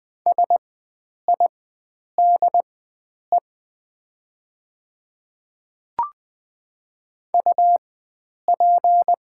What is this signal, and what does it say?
Morse code, heard as a keyed tone, about 20 wpm, spaced as for 9 wpm, 700 Hz. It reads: SIDE UP
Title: 4x Character Spacing - Top 300 Words - 20wpm